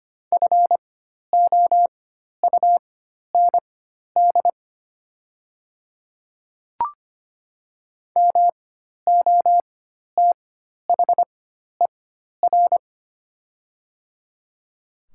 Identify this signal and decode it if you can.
Morse code, heard as a keyed tone, about 25 wpm, spaced as for 12 wpm, 690 Hz. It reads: FOUND MOTHER
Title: 4x Character Spacing - Top 300 Words - 25wpm